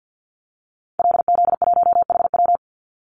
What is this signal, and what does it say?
Morse code, heard as a keyed tone, about 50 wpm, 700 Hz. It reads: LZ1HW
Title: Call Signs Repeated 3x - 50wpm